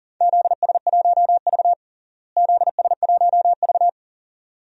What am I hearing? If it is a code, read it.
Morse code, heard as a keyed tone, about 40 wpm, 700 Hz. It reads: ZS1V ZS1V
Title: DXCC-100 Call Signs - 4 Characters - Repeated 2x - 40wpm